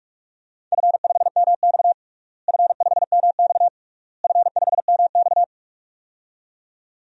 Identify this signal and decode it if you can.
Morse code, heard as a keyed tone, about 44 wpm, 700 Hz. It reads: F5MX F5MX F5MX
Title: DXCC-100 Call Signs - 4 Characters - Repeated 3x - 45wpm